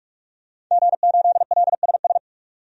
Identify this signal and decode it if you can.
Morse code, heard as a keyed tone, about 41 wpm, 700 Hz. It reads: G8LSS
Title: DXCC-100 Call Signs - 5 Characters - 45wpm